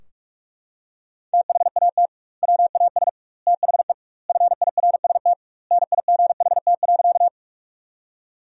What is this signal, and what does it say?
Morse code, heard as a keyed tone, about 41 wpm, 700 Hz. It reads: THAT WAS THE FIRST NIGHT.
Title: 2x Word-Spacing - Sentences from Top 300 Words - 45wpm